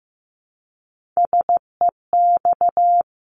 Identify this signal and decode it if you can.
Morse code, heard as a keyed tone, about 15 wpm, 700 Hz. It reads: SEX
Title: Top 1501-1750 Words 15wpm